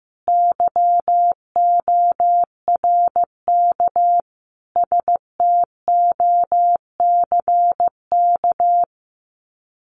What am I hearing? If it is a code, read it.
Morse code, heard as a keyed tone, about 15 wpm, 700 Hz. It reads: YORK STOCK
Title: Sets of 5 Words Encore 15wpm